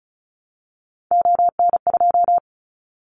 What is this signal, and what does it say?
Morse code, heard as a keyed tone, about 34 wpm, 700 Hz. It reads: ON2
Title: Callsign Prefixes for Top 25 DXCC - Repeated 3x - 35wpm